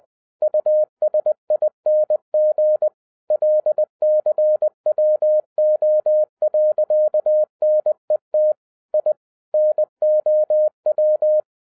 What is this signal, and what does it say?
Morse code, heard as a keyed tone, about 20 wpm, 610 Hz. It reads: USING LCWO.NET I NOW
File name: 2011-07-mp3 / n08 A LOT HAS CHANGED 20wpm